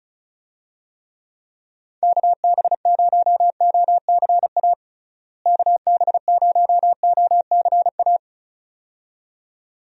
Morse code, heard as a keyed tone, about 34 wpm, 700 Hz. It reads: KB0OCA KB0OCA
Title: USA Call Signs - 6 Characters - Repeated 2x - 35wpm